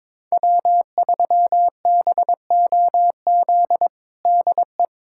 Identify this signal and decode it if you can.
Morse code, heard as a keyed tone, about 21 wpm, 700 Hz. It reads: W3BOZ DE
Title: USA Callsign DE Callsign - 5 Characters - 22wpm